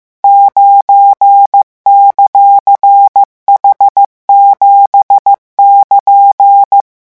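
Morse code, heard as a keyed tone, about 15 wpm, 800 Hz. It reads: 9;H7<KN>
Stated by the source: Morse code revision lesson 16 at 14 words per minute.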